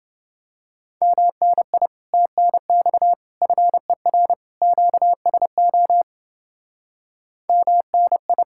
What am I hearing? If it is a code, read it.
Morse code, heard as a keyed tone, about 29 wpm, 700 Hz. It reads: MNI TNX FER QSO MNI